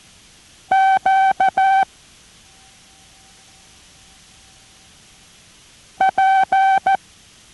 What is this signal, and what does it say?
Morse code, heard as a keyed tone, about 14 wpm, 770 Hz. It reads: Q P